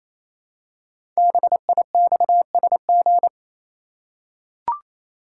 Morse code, heard as a keyed tone, about 27 wpm, 700 Hz. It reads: BIXSG
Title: Code Groups - 5 Letters - 28wpm